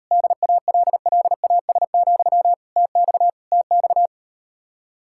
Morse code, heard as a keyed tone, about 38 wpm, 700 Hz. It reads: DALLAS, TX TX